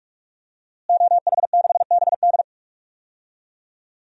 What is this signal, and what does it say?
Morse code, heard as a keyed tone, about 46 wpm, 700 Hz. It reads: OH6BD
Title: DXCC-100 Call Signs - 5 Characters - Repeated 3x - 45wpm